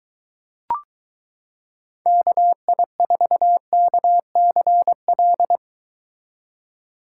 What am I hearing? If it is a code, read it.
Morse code, heard as a keyed tone, about 23 wpm, 700 Hz. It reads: KI4KCL